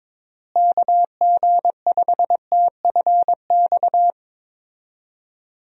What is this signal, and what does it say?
Morse code, heard as a keyed tone, about 22 wpm, 700 Hz. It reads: KG5TFX
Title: USA Call Signs - 6 Characters - Repeated 2x - 22wpm